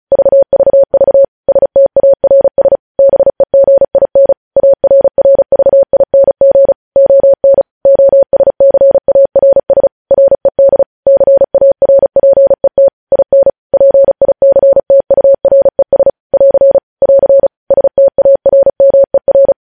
Morse code, heard as a keyed tone, about 36 wpm, 550 Hz. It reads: VVV STARS BEGIN ARRVING ON OSCARS RED CARPET IN PICTURES <AR> <AR> STARMER
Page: News Headlines in Morse Code at 35 WPM